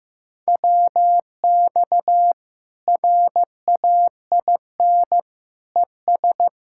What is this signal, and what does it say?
Morse code, heard as a keyed tone, about 15 wpm, 700 Hz. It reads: WX RAIN ES
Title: Practice QSO - Exchange 2 - Info - 15wpm